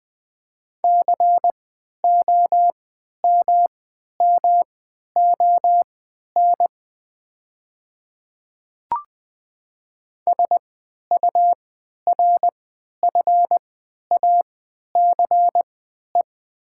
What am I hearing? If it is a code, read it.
Morse code, heard as a keyed tone, about 20 wpm, spaced as for 11 wpm, 700 Hz. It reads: COMMON SURFACE
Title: 3x Character Spacing - Top 500 Words - 20wpm